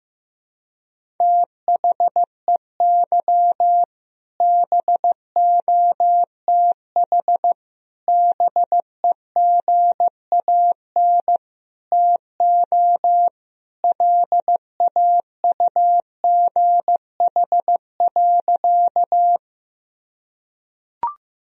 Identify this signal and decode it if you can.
Morse code, heard as a keyed tone, about 16 wpm, 700 Hz. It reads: THEY BOTH BEGAN TO LAUGH.
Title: Sentences from Top 1000 Words - Review Part 1 - 15wpm